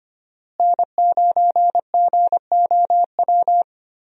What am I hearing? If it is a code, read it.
Morse code, heard as a keyed tone, about 25 wpm, 700 Hz. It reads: N9GOW
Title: USA Call Signs - 5 Characters - Repeated 2x - 25wpm